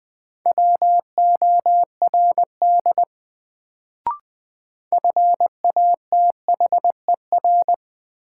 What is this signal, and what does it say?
Morse code, heard as a keyed tone, about 20 wpm, 700 Hz. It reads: WORD FATHER
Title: Sentences from Top 300 Words - New Word Drill - 20wpm